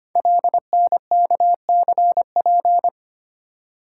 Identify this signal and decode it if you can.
Morse code, heard as a keyed tone, about 25 wpm, 700 Hz. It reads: LNKCP
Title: Code Groups - 5 Letter:Numbers - 25wpm